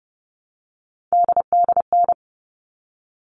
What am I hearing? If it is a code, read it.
Morse code, heard as a keyed tone, about 30 wpm, 700 Hz. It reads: DDN
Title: All Cut Numbers - 3-Digit 30wpm